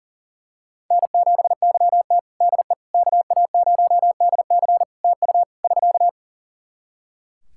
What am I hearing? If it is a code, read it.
Morse code, heard as a keyed tone, about 37 wpm, 700 Hz. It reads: N7YT DE KA0DC TU <SK>
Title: Practice QSO - Exchange 4 - Ending - 40wpm